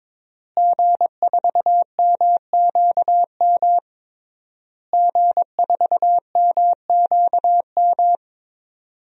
Morse code, heard as a keyed tone, about 23 wpm, 700 Hz. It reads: G4MQM G4MQM
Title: DXCC-100 Call Signs - 5 Characters - Repeated 3x - 22wpm